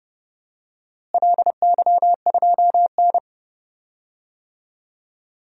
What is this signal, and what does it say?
Morse code, heard as a keyed tone, about 31 wpm, 700 Hz. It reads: LY2N
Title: DXCC-100 Call Signs - 4 Characters - Repeated 2x - 30wpm